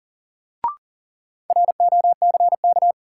Morse code, heard as a keyed tone, about 40 wpm, 700 Hz. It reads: ROCK
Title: Sentences from Top 400 Words - New Word Drill - 40wpm